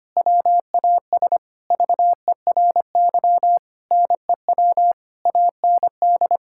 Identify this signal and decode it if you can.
Morse code, heard as a keyed tone, about 25 wpm, 700 Hz. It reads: WAS VERY NEW AND
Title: Sentences from Top 1000 Words - Review Part 6 - 25wpm